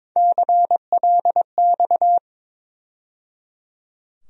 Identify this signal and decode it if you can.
Morse code, heard as a keyed tone, about 22 wpm, 700 Hz. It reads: CLX
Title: USA Call Sign Postfixes - 3 Letters 22wpm